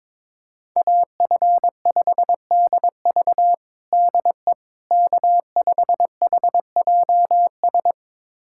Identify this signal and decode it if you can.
Morse code, heard as a keyed tone, about 22 wpm, 700 Hz. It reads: AF5DV DE K5HJS
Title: USA Callsign DE Callsign - 5 Characters - 22wpm